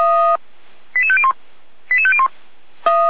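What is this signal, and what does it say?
Five-tone selective call (ZVEI), heard as quick runs of five tones, tones of about 70 ms. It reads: 88501 88501
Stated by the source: ZVEI five-tone selective call received on an SDR.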